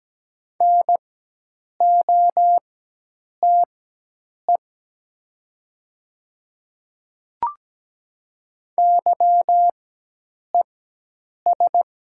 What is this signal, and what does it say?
Morse code, heard as a keyed tone, about 17 wpm, spaced as for 8 wpm, 700 Hz. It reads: NOTE YES
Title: 4x Character Spacing - Top 500 Words - 17wpm